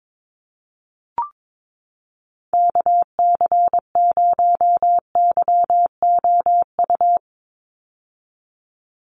Morse code, heard as a keyed tone, about 22 wpm, 700 Hz. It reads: KC0YOU
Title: USA Call Signs 22wpm